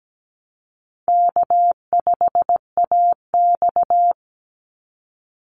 Morse code, heard as a keyed tone, about 17 wpm, 700 Hz. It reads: K5AX
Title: Easy Call Signs - 17wpm